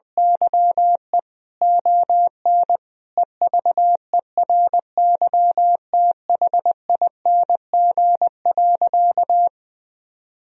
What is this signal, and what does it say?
Morse code, heard as a keyed tone, about 21 wpm, 700 Hz. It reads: YE ON EVERYTHING.
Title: Sentences from Top 1000 Words - Review Part 1 - 20wpm